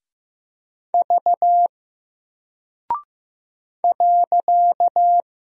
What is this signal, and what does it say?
Morse code, heard as a keyed tone, about 15 wpm, 700 Hz. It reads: V .
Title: Single Character (Letter:Number:Punctuation:Prosign) Alt 15wpm